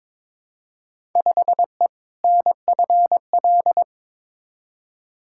Morse code, heard as a keyed tone, about 22 wpm, 700 Hz. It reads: 5E NFL
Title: Field Day Exchanges Encore - 22wpm